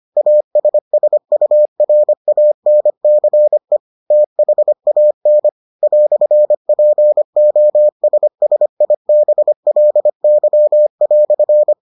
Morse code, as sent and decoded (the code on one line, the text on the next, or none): .- ... ... ..- .-. .- -. -.-. / - .... .- -. / .-..-. .--. --- ... ... .. -... .-.. -.-- .-..-.
ASSURANCE THAN "POSSIBLY"